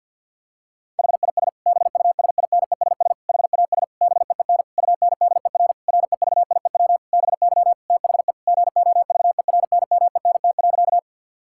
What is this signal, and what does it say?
Morse code, heard as a keyed tone, about 50 wpm, 700 Hz. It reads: HIS BUSINESS HAS BEEN UNDER REVIEW BY THE GOVERNMENT.
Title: Sentences from Top 1000 Words - Review Part 6 - 50wpm